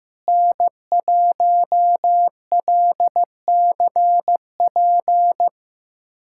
Morse code, heard as a keyed tone, about 16 wpm, 700 Hz. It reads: N1LCP
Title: USA Callsign DE Callsign - 5 Characters - 15wpm